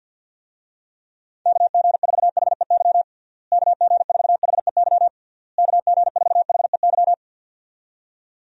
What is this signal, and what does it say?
Morse code, heard as a keyed tone, about 52 wpm, 700 Hz. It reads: KG4HEY KG4HEY KG4HEY
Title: USA Call Signs - 6 Characters - Repeated 3x - 50wpm